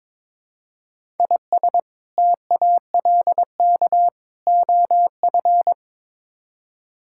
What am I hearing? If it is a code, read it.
Morse code, heard as a keyed tone, about 22 wpm, 700 Hz. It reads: IS TALK OF
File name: practice / 22wpm / Build A Phrase - 5-Word Phrases - 22wpm